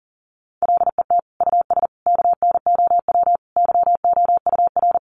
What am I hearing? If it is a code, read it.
Morse code, heard as a keyed tone, about 40 wpm, 700 Hz. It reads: LET US KNOW YOUR
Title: Sentences from Top 1400 Words 40wpm